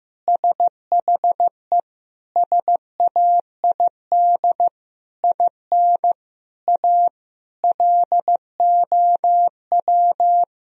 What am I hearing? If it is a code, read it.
Morse code, heard as a keyed tone, about 15 wpm, 700 Hz. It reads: SHE SAID IN A LOW
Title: Sentences from Top 1000 Words - Review Part 1 - 15wpm